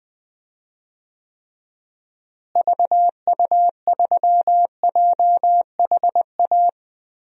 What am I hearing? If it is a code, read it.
Morse code, heard as a keyed tone, about 21 wpm, 700 Hz. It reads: VU3JHA
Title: DXCC-100 Call Signs - 6 Characters - Repeated 2x - 20wpm